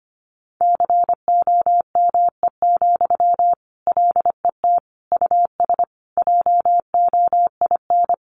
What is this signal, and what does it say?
Morse code, heard as a keyed tone, about 25 wpm, 700 Hz. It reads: COME, LET US JOIN
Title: Sentences from Top 1400 Words 25wpm